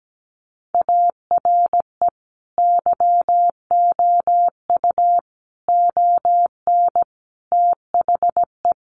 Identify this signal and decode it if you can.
Morse code, heard as a keyed tone, about 17 wpm, 700 Hz. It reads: ARE YOU ON THE
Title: Sentences from Top 1100 Words 17wpm